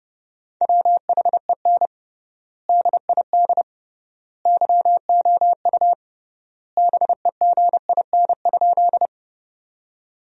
Morse code, heard as a keyed tone, about 31 wpm, 700 Hz. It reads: WHEN DID YOU BEGIN?
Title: 3x Word-Spacing - Sentences from Top 300 Words - 30wpm